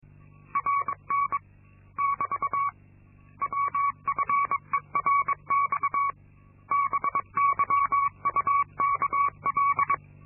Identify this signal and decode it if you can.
Morse code, heard as a keyed tone, about 22 wpm, 1120 Hz. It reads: RN <BT> WFERX BYUKL
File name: Linksmakalnis-2